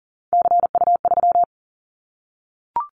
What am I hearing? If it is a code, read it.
Morse code, heard as a keyed tone, about 42 wpm, 690 Hz. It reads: CU3
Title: Callsign Prefixes for Top 100 DXCC - 40wpm